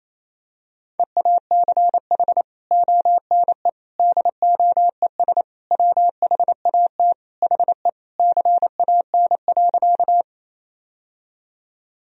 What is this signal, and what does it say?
Morse code, heard as a keyed tone, about 27 wpm, 700 Hz. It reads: EACH ONE DOES WHAT HE CAN.
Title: Sentences from Top 1000 Words - Review Part 1 - 28wpm